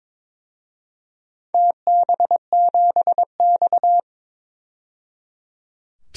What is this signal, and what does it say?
Morse code, heard as a keyed tone, about 22 wpm, 700 Hz. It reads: TB7X
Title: DXCC-100 Call Signs - 4 Characters - 22wpm